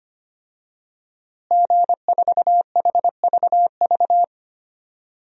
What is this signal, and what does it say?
Morse code, heard as a keyed tone, about 25 wpm, 700 Hz. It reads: G4HVV
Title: DXCC-100 Call Signs - 5 Characters - Repeated 3x - 25wpm